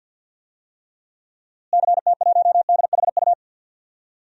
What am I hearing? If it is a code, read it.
Morse code, heard as a keyed tone, about 48 wpm, 700 Hz. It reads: CT1DHU
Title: DXCC-100 Call Signs - 6 Characters - Repeated 3x - 50wpm